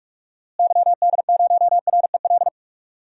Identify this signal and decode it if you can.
Morse code, heard as a keyed tone, about 46 wpm, 700 Hz. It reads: YD0REL
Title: DXCC-100 Call Signs - 6 Characters - Repeated 2x - 45wpm